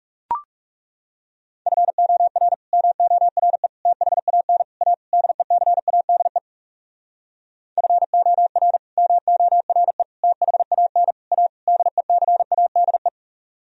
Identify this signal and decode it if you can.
Morse code, heard as a keyed tone, about 40 wpm, 700 Hz. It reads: FOR MORE THAN A DECADE FOR MORE THAN A DECADE